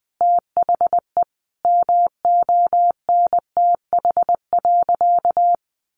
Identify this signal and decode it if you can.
Morse code, heard as a keyed tone, about 20 wpm, 700 Hz. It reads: THE MONTH.
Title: Sentences from Top 800 Words 20wpm